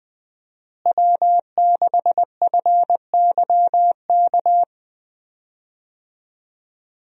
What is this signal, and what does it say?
Morse code, heard as a keyed tone, about 20 wpm, 700 Hz. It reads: W6FYK
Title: USA Call Signs - 5 Characters - Repeated 2x - 20wpm